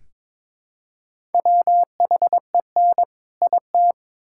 Morse code, heard as a keyed tone, about 22 wpm, 700 Hz. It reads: WHEN IT
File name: Build A Phrase - 2-Word Phrases - 22wpm